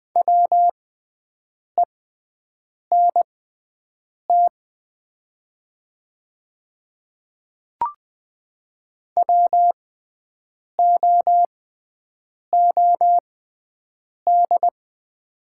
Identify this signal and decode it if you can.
Morse code, heard as a keyed tone, about 20 wpm, spaced as for 7 wpm, 700 Hz. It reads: WENT WOOD
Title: 6x Character Spacing - Top 300 Words - 20wpm